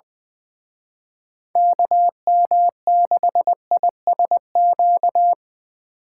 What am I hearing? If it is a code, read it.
Morse code, heard as a keyed tone, about 23 wpm, 700 Hz. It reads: KM6ISQ
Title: Speed Racing - USA 2x3 Call Signs 20wpm